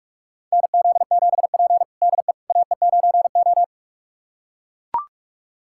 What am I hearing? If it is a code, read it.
Morse code, heard as a keyed tone, about 44 wpm, 700 Hz. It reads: NZ7P DE AE9O
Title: USA Callsign DE Callsign - 4 Characters - 45wpm